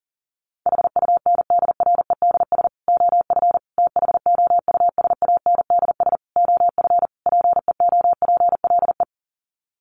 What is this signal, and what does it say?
Morse code, heard as a keyed tone, about 38 wpm, 700 Hz. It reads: HUNDREDS OF THOUSANDS OF PEOPLE
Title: Sets of 5 Words Encore2 40wpm